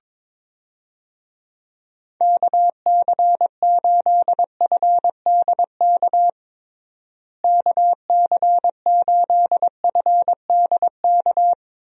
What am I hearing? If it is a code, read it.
Morse code, heard as a keyed tone, about 22 wpm, 700 Hz. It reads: KC8FDK KC8FDK
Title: USA Call Signs - 6 Characters - Repeated 3x - 22wpm